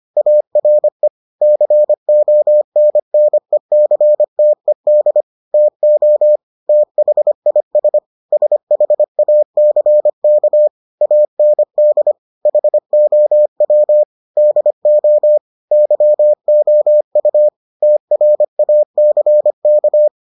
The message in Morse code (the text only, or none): ARE CONNECTED TO THIS SHACK AND HOW DO YOU TRACK